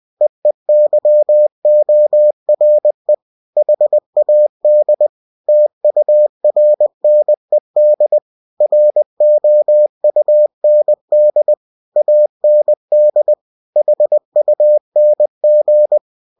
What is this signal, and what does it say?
Morse code, heard as a keyed tone, about 20 wpm, 600 Hz. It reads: EEYORE HAD TURNED ROUND AND HUNG
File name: winnie-20wpm